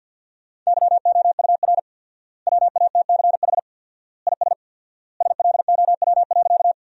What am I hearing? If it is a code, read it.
Morse code, heard as a keyed tone, about 50 wpm, 700 Hz. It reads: YOUR WATCH IS SLOW.
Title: 4x Word-Spacing - Sentences from Top 400 Words - 50wpm